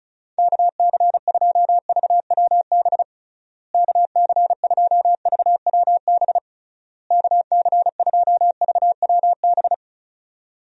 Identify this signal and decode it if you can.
Morse code, heard as a keyed tone, about 34 wpm, 700 Hz. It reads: KC2VWB KC2VWB KC2VWB
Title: USA Call Signs - 6 Characters - Repeated 3x - 35wpm